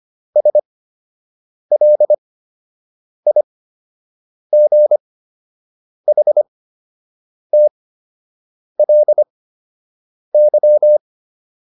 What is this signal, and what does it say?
Morse code, heard as a keyed tone, about 25 wpm, spaced as for 7 wpm, 600 Hz. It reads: SLIGHTLY